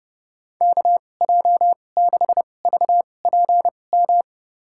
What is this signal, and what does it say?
Morse code, heard as a keyed tone, about 30 wpm, spaced as for 22 wpm, 700 Hz. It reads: KJ6VPM
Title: USA Call Signs - 6 Characters - 2x Character Spacing - 30wpm